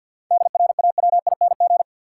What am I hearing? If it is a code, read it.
Morse code, heard as a keyed tone, about 52 wpm, 700 Hz. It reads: DRAWING
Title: Seven Letter Words - 50wpm